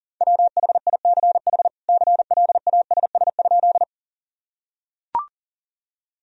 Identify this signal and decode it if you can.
Morse code, heard as a keyed tone, about 40 wpm, 700 Hz. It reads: WHICH CLASS?
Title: Sentences from Top 1000 Words - Review Part 2 - 40wpm